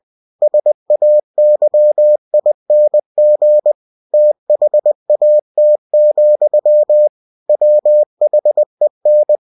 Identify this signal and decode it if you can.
Morse code, heard as a keyed tone, about 20 wpm, 600 Hz. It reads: SAYING THAT, WHEN